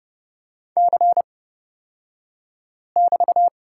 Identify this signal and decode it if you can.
Morse code, heard as a keyed tone, about 31 wpm, spaced as for 5 wpm, 700 Hz. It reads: C<BT>
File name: Lesson 36d - BT - Two Characters 30wpm